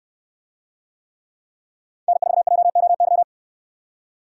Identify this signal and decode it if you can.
Morse code, heard as a keyed tone, about 69 wpm, 700 Hz. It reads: N4JOY